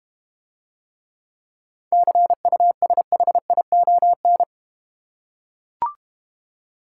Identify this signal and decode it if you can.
Morse code, heard as a keyed tone, about 31 wpm, 700 Hz. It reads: CUSHION
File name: Easy Words - Speed-Racing with Hints - 25wpm